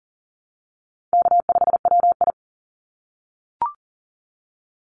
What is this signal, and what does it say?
Morse code, heard as a keyed tone, about 40 wpm, 700 Hz. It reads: K5WI
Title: NA SOTA Chaser Callsigns 40wpm